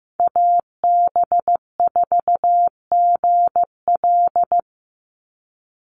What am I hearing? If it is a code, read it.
Morse code, heard as a keyed tone, about 15 wpm, 700 Hz. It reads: AB4GL
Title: USA Call Signs 15wpm